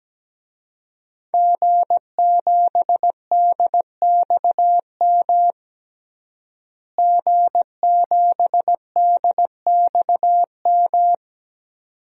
Morse code, heard as a keyed tone, about 17 wpm, 700 Hz. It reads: G7DXM G7DXM
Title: DXCC-100 Call Signs - 5 Characters - Repeated 2x - 17wpm